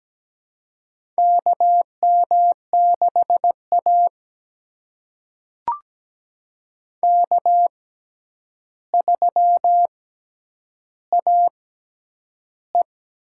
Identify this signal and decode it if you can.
Morse code, heard as a keyed tone, about 17 wpm, spaced as for 9 wpm, 700 Hz. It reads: KM6A K3AE
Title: USA Call Signs - 6x Character Spacing - 17wpm